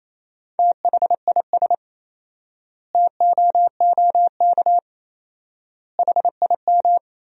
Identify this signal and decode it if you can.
Morse code, heard as a keyed tone, about 28 wpm, 700 Hz. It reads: THIS TOOK HIM
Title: 4x Word-Spacing - Sentences from Top 300 Words - 28wpm